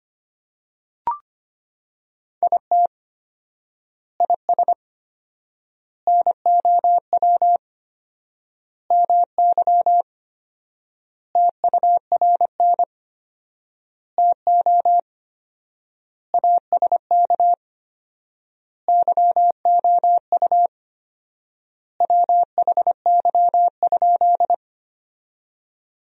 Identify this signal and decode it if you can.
Morse code, heard as a keyed tone, about 25 wpm, 700 Hz. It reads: IT IS NOW MY TURN TO ASK YOU WHY?
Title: 4x Word-Spacing - Sentences from Top 200 Words - 25wpm